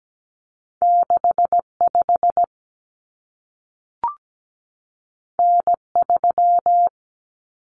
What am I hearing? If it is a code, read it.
Morse code, heard as a keyed tone, about 17 wpm, 700 Hz. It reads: 65 N3
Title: Common Cut Numbers - 2-Digit 17wpm